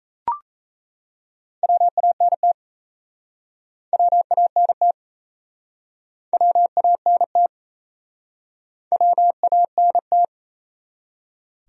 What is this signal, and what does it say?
Morse code, heard as a keyed tone, about 38 wpm, 700 Hz. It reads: WANT WANT WANT WANT